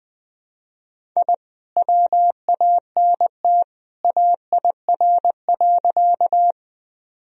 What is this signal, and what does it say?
Morse code, heard as a keyed tone, about 20 wpm, 700 Hz. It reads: I WANT AIR.
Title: Sentences from Top 1000 Words - Review Part 1 - 20wpm